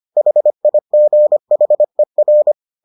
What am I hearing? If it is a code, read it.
Morse code, heard as a keyed tone, about 25 wpm, 600 Hz. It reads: HIGHER